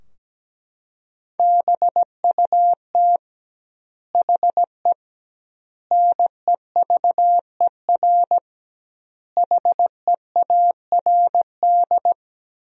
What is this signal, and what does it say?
Morse code, heard as a keyed tone, about 17 wpm, 700 Hz. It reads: BUT HE NEVER HEARD
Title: 2x Word-Spacing - Sentences from Top 400 Words - 17wpm